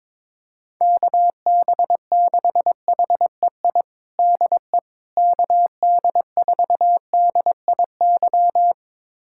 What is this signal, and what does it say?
Morse code, heard as a keyed tone, about 22 wpm, 700 Hz. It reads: KB6HEI DE KD4DIY
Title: USA Callsign DE Callsign - 6 Characters - 22wpm